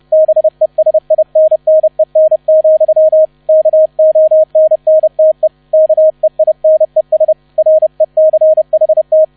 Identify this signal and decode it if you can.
Morse code, heard as a keyed tone, about 30 wpm, 630 Hz, with radio noise. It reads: BESINNEN, KONNTE KEINES RECHT